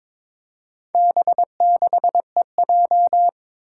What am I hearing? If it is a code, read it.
Morse code, heard as a keyed tone, about 23 wpm, 700 Hz. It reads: B6EJ
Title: Four Challenging Letter-Number Combinations 22wpm